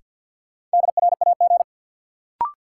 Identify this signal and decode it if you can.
Morse code, heard as a keyed tone, about 53 wpm, 700 Hz. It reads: DRAG
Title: Sentences from Top 1500 Words - New Word Drill - 50wpm